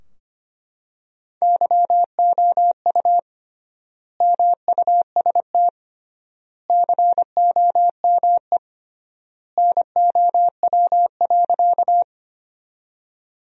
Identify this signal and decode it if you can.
Morse code, heard as a keyed tone, about 25 wpm, 700 Hz. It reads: YOU MUST COME NOW.
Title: 3x Word-Spacing - Sentences from Top 200 Words - 25wpm